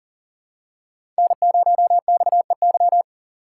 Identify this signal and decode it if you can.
Morse code, heard as a keyed tone, about 40 wpm, 700 Hz. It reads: N0XEY
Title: USA Call Signs - 5 Characters - 4x Character Spacing - 40wpm